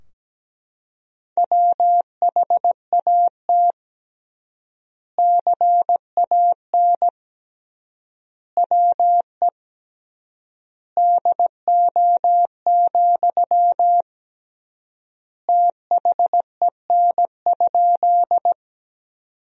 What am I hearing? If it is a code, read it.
Morse code, heard as a keyed tone, about 17 wpm, 700 Hz. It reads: WHAT CAN WE DO, THEN?
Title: 3x Word-Spacing - Sentences from Top 100 Words - 17wpm